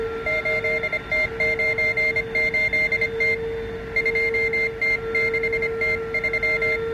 Morse code, heard as a keyed tone, about 26 wpm, 2100 Hz, with noise. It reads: 8T98T 2T6T3